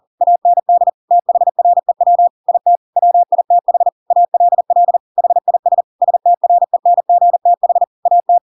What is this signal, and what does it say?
Morse code, heard as a keyed tone, about 40 wpm, 700 Hz. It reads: AND THREW IT WITH ALL HIS STRENGTH AT
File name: Pinocchio / Pinocchio ch4 40 40 ik1hge